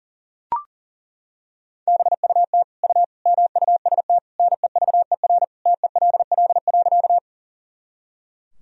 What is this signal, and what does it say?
Morse code, heard as a keyed tone, about 40 wpm, 700 Hz. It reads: BUT U MUST NEVER TELL.
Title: Sentences from Top 300 Words and Abbreviations 40wpm